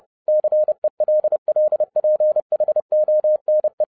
Morse code, heard as a keyed tone, about 30 wpm, 610 Hz. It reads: CELLPHONE